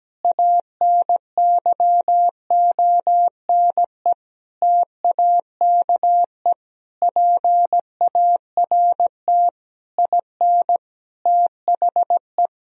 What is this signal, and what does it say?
Morse code, heard as a keyed tone, about 17 wpm, 700 Hz. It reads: ANYONE TAKE PART IN THE
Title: Sentences from Top 1000 Words - Review Part 4 - 17wpm